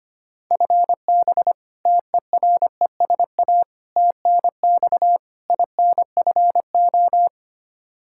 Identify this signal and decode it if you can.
Morse code, heard as a keyed tone, about 25 wpm, 700 Hz. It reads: FB TERESA TNX INFO
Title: Practice QSO - Exchange 3 - More Info - 25wpm